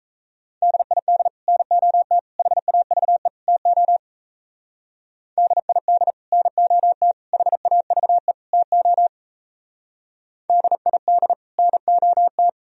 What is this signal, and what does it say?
Morse code, heard as a keyed tone, about 33 wpm, 700 Hz. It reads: DID NOT HAVE TO DID NOT HAVE TO DID NOT
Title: Speed-Racing - Sets of 4 Words 28wpm